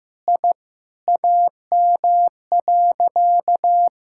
Morse code, heard as a keyed tone, about 15 wpm, 700 Hz. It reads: I AM.
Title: Sentences from Top 1000 Words - Review Part 6 - 15wpm